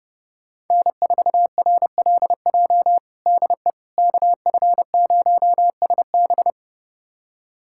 Morse code, heard as a keyed tone, about 32 wpm, 700 Hz. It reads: N4RLJ DE KF0SB
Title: USA Callsign DE Callsign - 5 Characters - 30wpm